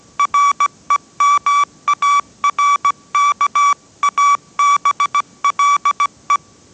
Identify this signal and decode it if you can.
Morse code, heard as a keyed tone, about 17 wpm, 1150 Hz. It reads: REMARKABLE
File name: 18wpm